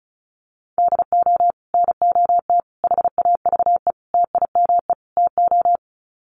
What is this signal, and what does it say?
Morse code, heard as a keyed tone, about 34 wpm, 700 Hz. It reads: DO NOT HAVE TIME TO
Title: Sets of 5 Words 35wpm